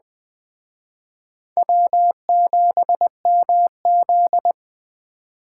Morse code, heard as a keyed tone, about 20 wpm, 700 Hz. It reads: W7MZ
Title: Speed Racing - USA 1x2 Call Signs 17wpm